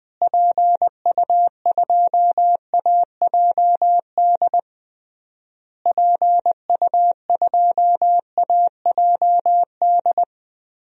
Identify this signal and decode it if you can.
Morse code, heard as a keyed tone, about 20 wpm, 700 Hz. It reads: PU2AJD PU2AJD
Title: DXCC-100 Call Signs - 6 Characters - Repeated 3x - 20wpm